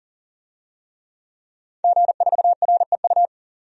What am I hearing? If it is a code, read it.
Morse code, heard as a keyed tone, about 40 wpm, 700 Hz. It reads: G4REU
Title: DXCC-100 Call Signs - 5 Characters - 40wpm